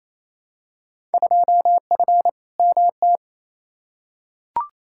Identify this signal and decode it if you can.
Morse code, heard as a keyed tone, about 28 wpm, 700 Hz. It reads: 2F MT
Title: Field Day Exchanges Encore - 28wpm